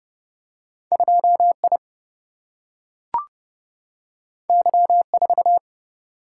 Morse code, heard as a keyed tone, about 30 wpm, 700 Hz. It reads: 2 I Y 4
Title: Single-Letter Single-Number 30wpm